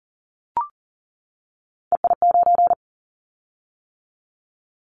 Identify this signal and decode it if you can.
Morse code, heard as a keyed tone, about 38 wpm, 700 Hz. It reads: EI9
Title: Callsign Prefixes for Top 50 DXCC - 40wpm